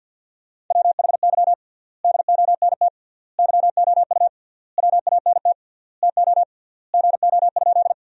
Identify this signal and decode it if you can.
Morse code, heard as a keyed tone, about 50 wpm, 700 Hz. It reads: WHY DONT YOU WANT TO GO?
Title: 3x Word-Spacing - Sentences from Top 300 Words - 50wpm